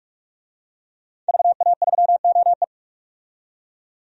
Morse code, heard as a keyed tone, about 44 wpm, 700 Hz. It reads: VA3OE
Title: DXCC-100 Call Signs - 5 Characters - 45wpm